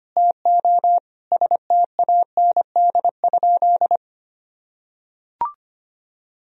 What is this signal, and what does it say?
Morse code, heard as a keyed tone, about 25 wpm, 700 Hz. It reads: TO STAND?
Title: Sentences from Top 1000 Words - Review Part 3 - 25wpm